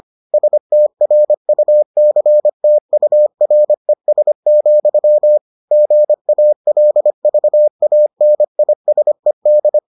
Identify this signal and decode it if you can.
Morse code, heard as a keyed tone, about 25 wpm, 600 Hz. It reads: STRUCTURES, GALVANISED